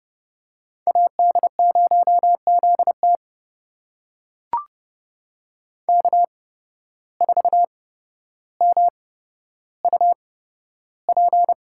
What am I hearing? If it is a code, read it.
Morse code, heard as a keyed tone, about 30 wpm, spaced as for 13 wpm, 700 Hz. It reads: AD0ZT K4MUP
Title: USA Call Signs - 8x Character Spacing - 30wpm